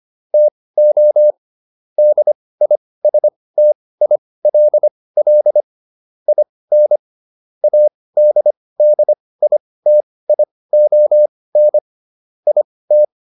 Morse code, heard as a keyed tone, about 25 wpm, spaced as for 18 wpm, 600 Hz. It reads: TO DISTILL IN ADDITION IT